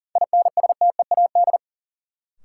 Morse code, heard as a keyed tone, about 40 wpm, 700 Hz. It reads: INSTEAD
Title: Sentences from Top 800 Words - New Word Drill - 40wpm